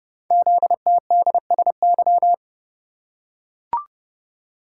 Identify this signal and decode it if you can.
Morse code, heard as a keyed tone, about 30 wpm, 700 Hz. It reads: ZTDSY
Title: Code Groups - 5 Letters - 30wpm